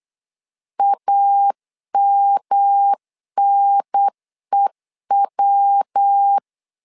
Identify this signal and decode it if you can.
Morse code, heard as a keyed tone, about 8 wpm, 800 Hz. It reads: AMNEW
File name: LEZIONI / 3 MWRO / 3 MW 8wpm 2501